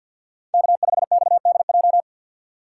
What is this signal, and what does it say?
Morse code, heard as a keyed tone, about 53 wpm, 700 Hz. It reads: K5XDJ
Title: USA Call Signs - 5 Characters - 3x Character Spacing - 50wpm